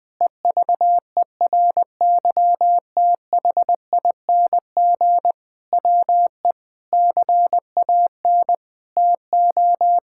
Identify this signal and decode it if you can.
Morse code, heard as a keyed tone, about 21 wpm, 700 Hz. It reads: EVERYTHING WE CAN TO
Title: Sentences with 8-Word Phrases - 20wpm